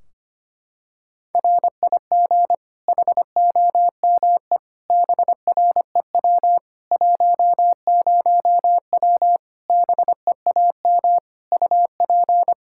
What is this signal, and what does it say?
Morse code, heard as a keyed tone, about 25 wpm, 700 Hz. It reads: RIG HOME BREW 10W BEAM UP
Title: Practice QSO - Exchange 2 - Info - 25wpm